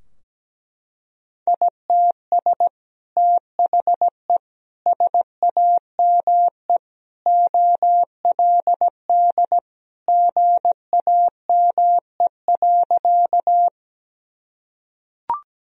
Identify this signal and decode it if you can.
Morse code, heard as a keyed tone, about 17 wpm, 700 Hz. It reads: ITS THE SAME OLD GAME.
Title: Sentences from Top 1000 Words - Review Part 3 - 17wpm